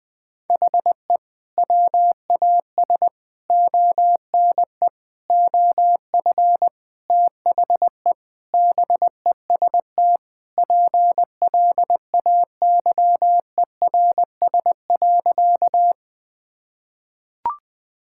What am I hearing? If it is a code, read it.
Morse code, heard as a keyed tone, about 20 wpm, 700 Hz. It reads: HE WAS ONE OF THE BEST PLAYERS.
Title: Sentences with 5-Word Phrases - Spoken Beginning - 20wpm